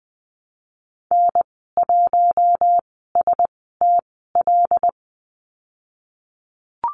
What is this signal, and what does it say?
Morse code, heard as a keyed tone, about 19 wpm, spaced as for 14 wpm, 700 Hz. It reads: N1STL
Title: 088 Lesson 15g - Number 5 - Call Signs 20wpm